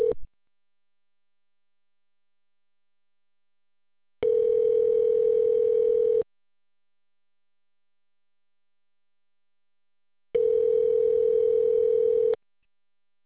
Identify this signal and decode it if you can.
Ringback tone (North America), 440 + 480 Hz.